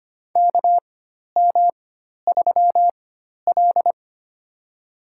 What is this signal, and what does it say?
Morse code, heard as a keyed tone, about 25 wpm, spaced as for 12 wpm, 700 Hz. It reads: KM3L
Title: USA Call Signs - 4 Characters - 4x Character Spacing - 25wpm